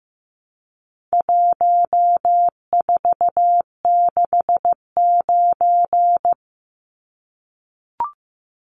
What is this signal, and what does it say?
Morse code, heard as a keyed tone, about 15 wpm, 700 Hz. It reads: 1469
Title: Numbers - 4-Digit 15wpm